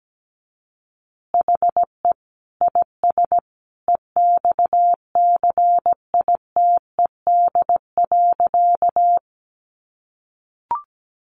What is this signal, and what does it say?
Morse code, heard as a keyed tone, about 17 wpm, 700 Hz. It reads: HE IS EXCITED.
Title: Additional Easy Phrases 17wpm